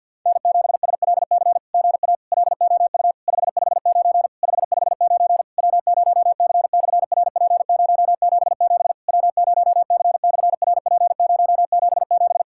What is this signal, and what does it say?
Morse code, heard as a keyed tone, about 50 wpm, 700 Hz. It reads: N7SLC GA LOU 559 559 W0C/RP077 W0C/RP077
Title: SOTA QSO Practice 50wpm